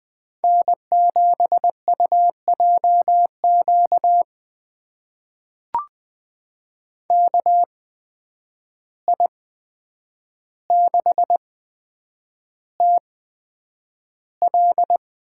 Morse code, heard as a keyed tone, about 20 wpm, spaced as for 9 wpm, 700 Hz. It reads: N7UJQ KI6TL
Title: USA Call Signs - 5 Characters - 8x Character Spacing - 20wpm